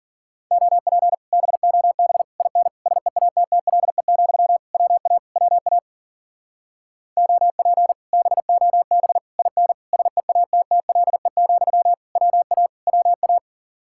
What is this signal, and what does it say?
Morse code, heard as a keyed tone, about 48 wpm, 700 Hz. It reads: OP BOB IN SEATTLE, WA WA OP BOB IN SEATTLE, WA WA
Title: Speed-Racing Common QSO Phrases - 40wpm